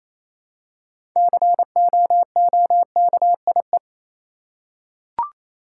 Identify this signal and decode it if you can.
Morse code, heard as a keyed tone, about 28 wpm, 700 Hz. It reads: COOKIE
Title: Six Letter Words - 28wpm